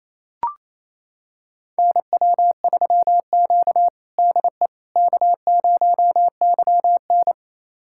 Morse code, heard as a keyed tone, about 28 wpm, 700 Hz. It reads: NW3Q DE K0YN
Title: USA Callsign DE Callsign - 4 Characters - 28wpm